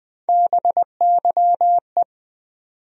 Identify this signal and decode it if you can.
Morse code, heard as a keyed tone, about 20 wpm, 700 Hz. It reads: BYE
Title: USA Call Sign Postfixes - 3 Letters 20wpm